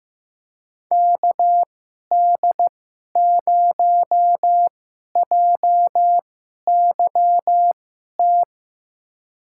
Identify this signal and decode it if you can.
Morse code, heard as a keyed tone, about 15 wpm, spaced as for 11 wpm, 700 Hz. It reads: KD0JYT
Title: USA Call Signs - 2x Character Spacing - 15wpm